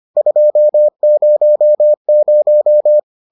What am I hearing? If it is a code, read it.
Morse code, heard as a keyed tone, about 25 wpm, 600 Hz. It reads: 200